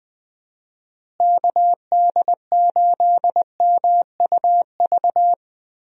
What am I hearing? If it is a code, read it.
Morse code, heard as a keyed tone, about 20 wpm, 700 Hz. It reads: KD8MUV
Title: USA Call Signs - 6 Characters - Repeated 3x - 20wpm